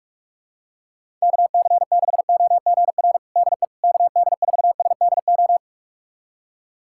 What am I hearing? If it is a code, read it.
Morse code, heard as a keyed tone, about 43 wpm, 700 Hz. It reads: KC6OGR DE KD4SDO
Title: USA Callsign DE Callsign 45wpm